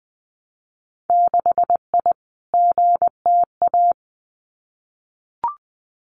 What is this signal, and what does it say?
Morse code, heard as a keyed tone, about 20 wpm, 700 Hz. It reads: 6I GTA
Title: Winter Field Day Exchanges Encore 20wpm